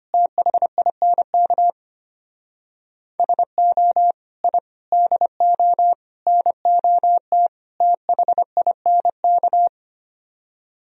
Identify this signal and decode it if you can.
Morse code, heard as a keyed tone, about 33 wpm, 700 Hz. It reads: THINK SO I DO NOT THINK